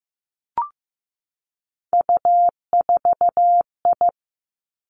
Morse code, heard as a keyed tone, about 15 wpm, 700 Hz. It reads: U4I
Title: Three Challenging Letter-Number Combinations 15wpm